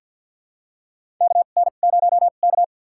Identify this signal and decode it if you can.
Morse code, heard as a keyed tone, about 50 wpm, spaced as for 36 wpm, 700 Hz. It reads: KN0K
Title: USA Call Signs - 4 Characters - 2x Character Spacing - 50wpm